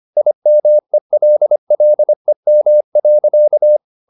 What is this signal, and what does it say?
Morse code, heard as a keyed tone, about 25 wpm, 600 Hz. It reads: IMELLEM.
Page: Ferie 25 wpm